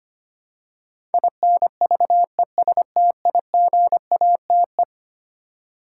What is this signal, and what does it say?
Morse code, heard as a keyed tone, about 25 wpm, 700 Hz. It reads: INVESTIGATE
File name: practice / 25wpm / Eleven Letter Words - 25wpm